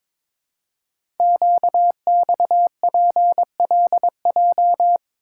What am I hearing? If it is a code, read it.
Morse code, heard as a keyed tone, about 22 wpm, 700 Hz. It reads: QXPLJ
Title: Code Groups - 5 Letters - 22wpm